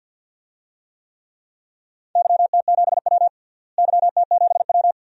Morse code, heard as a keyed tone, about 52 wpm, 700 Hz. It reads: YT7W YT7W
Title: DXCC-100 Call Signs - 4 Characters - Repeated 2x - 50wpm